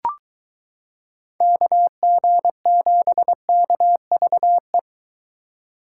Morse code, heard as a keyed tone, about 23 wpm, 700 Hz. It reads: KG7KVE